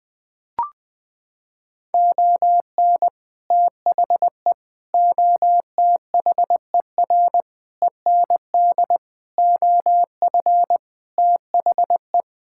Morse code, heard as a keyed tone, about 21 wpm, 700 Hz. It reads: ON THE OTHER END OF THE
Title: Sentences with 7-Word Phrases - 20wpm